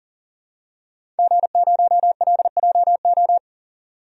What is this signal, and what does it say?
Morse code, heard as a keyed tone, about 40 wpm, 700 Hz. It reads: G0LJO
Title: DXCC-100 Call Signs - 5 Characters - 40wpm